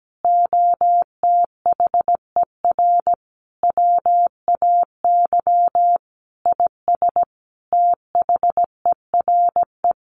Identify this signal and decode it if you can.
Morse code, heard as a keyed tone, about 17 wpm, 700 Hz. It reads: OTHER WAY IS THERE
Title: Sentences from Top 100 Words 17wpm